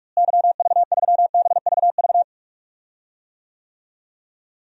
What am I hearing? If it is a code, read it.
Morse code, heard as a keyed tone, about 46 wpm, 700 Hz. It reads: YV3BVV
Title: DXCC-100 Call Signs - 6 Characters - Repeated 3x - 45wpm